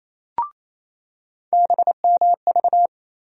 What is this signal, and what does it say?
Morse code, heard as a keyed tone, about 28 wpm, 700 Hz. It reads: BMV
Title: USA Call Sign Postfixes - 3 Letters 28wpm